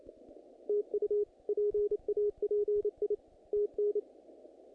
Morse code, heard as a keyed tone, about 29 wpm, 420 Hz. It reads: TU PAPI TN
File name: contests / naqp-cw-k5tr-op-k5pi- -1900